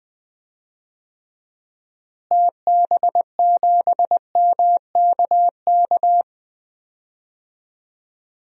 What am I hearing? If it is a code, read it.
Morse code, heard as a keyed tone, about 20 wpm, 700 Hz. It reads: TB7MKK
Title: DXCC-100 Call Signs - 6 Characters - Repeated 2x - 20wpm